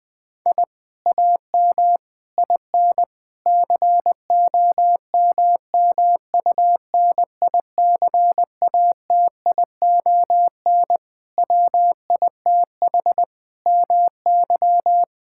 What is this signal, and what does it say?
Morse code, heard as a keyed tone, about 19 wpm, 700 Hz. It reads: I AM IN COMMUNICATION WITH MY
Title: Sentences from Top 1000 Words - Review Part 6 - 20wpm